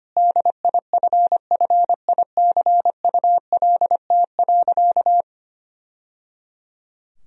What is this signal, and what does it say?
Morse code, heard as a keyed tone, about 25 wpm, 700 Hz. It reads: DIFFICULT.
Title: Sentences from Top 1000 Words - Review Part 3 - 25wpm